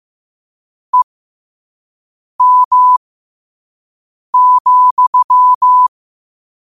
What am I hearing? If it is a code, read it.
Morse code, heard as a keyed tone, about 15 wpm, spaced as for 5 wpm, 1000 Hz. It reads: EM,
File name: qotdfiles / 5wpm